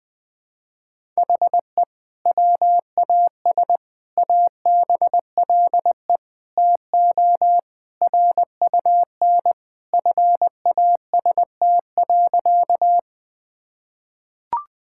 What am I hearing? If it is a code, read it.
Morse code, heard as a keyed tone, about 21 wpm, 700 Hz. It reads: HE WAS ABLE TO RUN FAST.
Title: Sentences with 2-Word Phrases - 20wpm